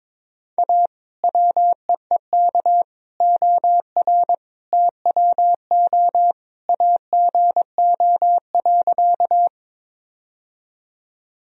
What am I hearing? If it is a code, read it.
Morse code, heard as a keyed tone, about 22 wpm, 700 Hz. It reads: A WEEK OR TWO AGO.
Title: Sentences from Top 1000 Words - Review Part 3 - 22wpm